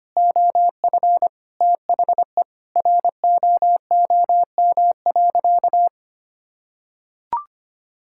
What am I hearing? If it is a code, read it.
Morse code, heard as a keyed tone, about 25 wpm, 700 Hz. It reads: OF THE ROOM.
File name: practice / 25wpm / Sentences with 2-Word Phrases - 25wpm